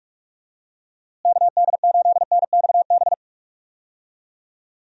Morse code, heard as a keyed tone, about 42 wpm, 700 Hz. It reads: KD8NXB
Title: USA Call Signs - 6 Characters - Repeated 3x - 45wpm